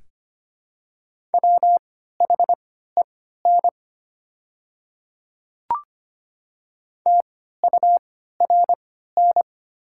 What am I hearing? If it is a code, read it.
Morse code, heard as a keyed tone, about 25 wpm, spaced as for 14 wpm, 700 Hz. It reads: WHEN TURN
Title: 3x Character Spacing - Top 200 Words - 25wpm